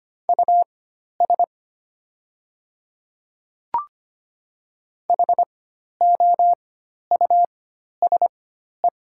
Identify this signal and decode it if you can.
Morse code, heard as a keyed tone, about 25 wpm, spaced as for 12 wpm, 700 Hz. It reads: US HOUSE
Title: 4x Character Spacing - Top 200 Words - 25wpm